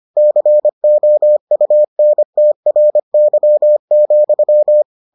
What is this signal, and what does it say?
Morse code, heard as a keyed tone, about 25 wpm, 600 Hz. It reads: COUNTRY,